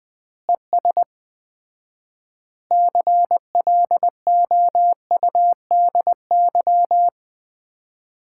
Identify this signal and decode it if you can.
Morse code, heard as a keyed tone, about 20 wpm, 700 Hz. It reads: ES CLOUDY
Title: Common QSO Phrases - 4x Word-Spacing - 20wpm